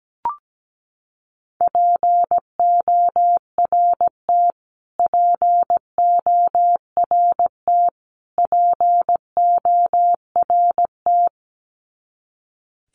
Morse code, heard as a keyed tone, about 17 wpm, 700 Hz. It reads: PORT PORT PORT
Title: Top 300 Words Repeated 17wpm